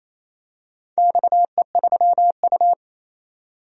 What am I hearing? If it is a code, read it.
Morse code, heard as a keyed tone, about 28 wpm, 700 Hz. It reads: XE3U
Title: DXCC-100 Call Signs - 4 Characters - 28wpm